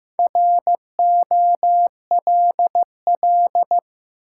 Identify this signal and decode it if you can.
Morse code, heard as a keyed tone, about 15 wpm, 700 Hz. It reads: ROLL
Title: Four Letter Words - 15wpm